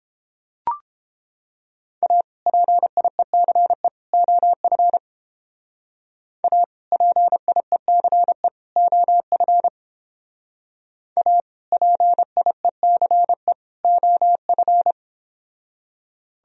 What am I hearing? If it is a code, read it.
Morse code, heard as a keyed tone, about 30 wpm, 700 Hz. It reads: A PIECE OF A PIECE OF A PIECE OF